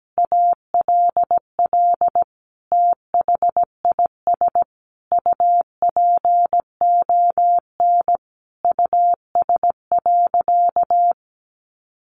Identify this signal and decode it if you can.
Morse code, heard as a keyed tone, about 17 wpm, 700 Hz. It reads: ALL THIS UPON US.
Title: Sentences from Top 1400 Words 17wpm